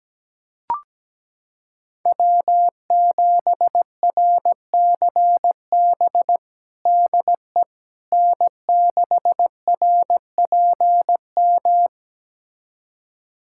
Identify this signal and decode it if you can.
Morse code, heard as a keyed tone, about 17 wpm, 700 Hz. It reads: W7RCB DE N6RPM
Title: USA Callsign DE Callsign - 5 Characters - 17wpm